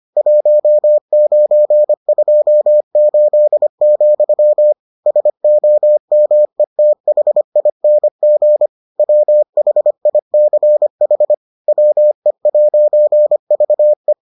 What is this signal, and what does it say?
Morse code, heard as a keyed tone, about 25 wpm, 600 Hz. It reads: 1928, SOMETHING WHICH WE'VE